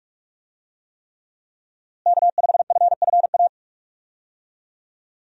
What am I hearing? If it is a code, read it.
Morse code, heard as a keyed tone, about 46 wpm, 700 Hz. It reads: K5FFA
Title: USA Call Signs - 5 Characters - Repeated 3x - 45wpm